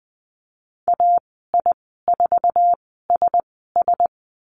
Morse code, heard as a keyed tone, about 20 wpm, spaced as for 14 wpm, 700 Hz. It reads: AI4SS
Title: 062 Lesson 11g - Letter h - Call Signs 20wpm